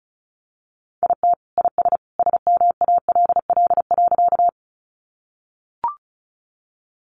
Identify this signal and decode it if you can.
Morse code, heard as a keyed tone, about 34 wpm, 700 Hz. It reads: IT IS SMALL.
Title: Easy Phrases 35wpm